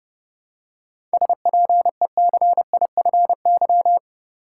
Morse code, heard as a keyed tone, about 30 wpm, 700 Hz. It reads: SPECIFY
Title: Seven Letter Words - 30wpm